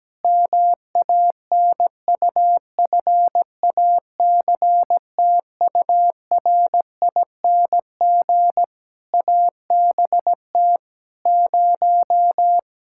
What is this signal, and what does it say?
Morse code, heard as a keyed tone, about 17 wpm, 700 Hz. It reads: MANUFACTURING ABT 0
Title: Practice QSO - Exchange 3 - More Info - 17wpm